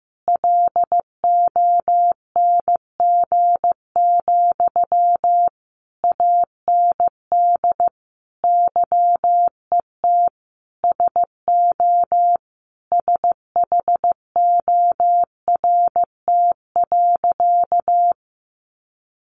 Morse code, heard as a keyed tone, about 15 wpm, 700 Hz. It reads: LONG, AND YET SO SHORT.
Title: Sentences from Top 500 Words 15wpm